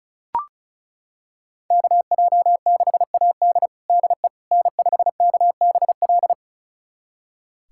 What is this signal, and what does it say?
Morse code, heard as a keyed tone, about 36 wpm, 700 Hz. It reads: KJ6AD DE N5KBL
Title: USA Callsign DE Callsign - 5 Characters - 35wpm